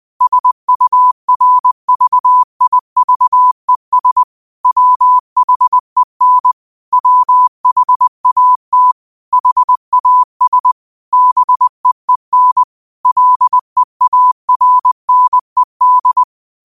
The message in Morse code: SURVIVES WHEN WHAT HAS BEEN LEARNED

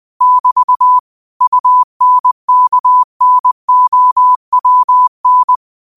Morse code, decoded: <BT> UNKNOWN

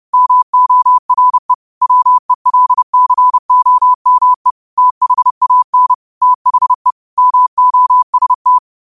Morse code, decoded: MORE WELCOME THAN THE MOST

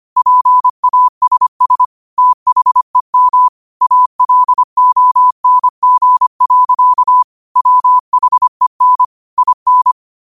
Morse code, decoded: PASS THEM ALONG. WHEN IN

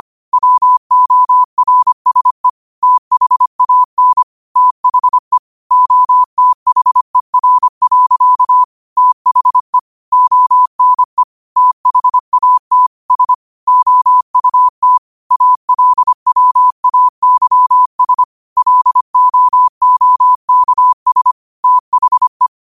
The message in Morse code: WORSE THAN THE OTHER. THE ONE THAT S OUT ALWAYS LOOKS THE

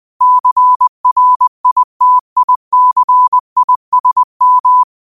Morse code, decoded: CRITICISM

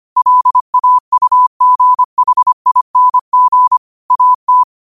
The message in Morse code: LAUGHING AT

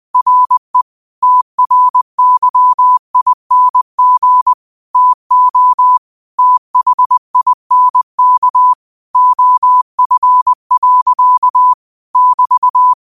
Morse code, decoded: RE TRYING TO THINK OF. <BT>

